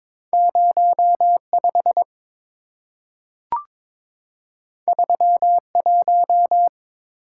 Morse code, heard as a keyed tone, about 22 wpm, 690 Hz. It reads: 05 31
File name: practice / 22wpm / Two Letter-Number Combinations - 22wpm